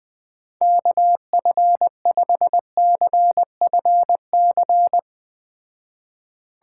Morse code, heard as a keyed tone, about 20 wpm, 700 Hz. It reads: KF5CFC